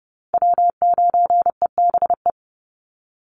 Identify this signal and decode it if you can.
Morse code, heard as a keyed tone, about 30 wpm, 700 Hz. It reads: W9EBE
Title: Call Signs Repeated 3x - 30wpm